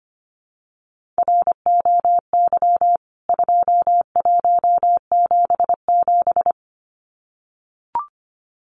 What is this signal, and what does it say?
Morse code, heard as a keyed tone, about 25 wpm, 700 Hz. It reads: ROY 2177
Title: CWT Exchange 25wpm